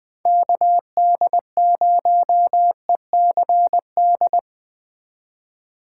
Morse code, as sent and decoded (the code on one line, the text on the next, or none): -.- -.. ----- . -.-. -..
KD0ECD